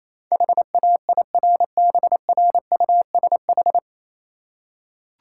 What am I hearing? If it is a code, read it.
Morse code, heard as a keyed tone, about 27 wpm, 700 Hz. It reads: HAIRBRUSH
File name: Easy Words - Speed-Racing with Hints - 22wpm